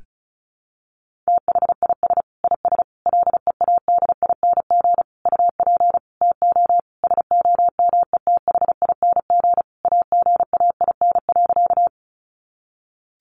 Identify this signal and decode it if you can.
Morse code, heard as a keyed tone, about 34 wpm, 700 Hz. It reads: THIS IS LEADING UP TO SOMETHING AGAIN.
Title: Sentences from Top 1100 Words 35wpm